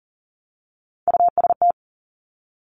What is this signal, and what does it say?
Morse code, heard as a keyed tone, about 38 wpm, 690 Hz. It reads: UST
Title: Top Three Letter Combinations in Words 40wpm